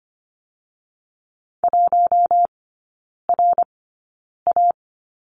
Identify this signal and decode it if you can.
Morse code, heard as a keyed tone, about 25 wpm, spaced as for 9 wpm, 700 Hz. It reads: 1RA
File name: Lesson 14e - Number 2 - Three Characters 25wpm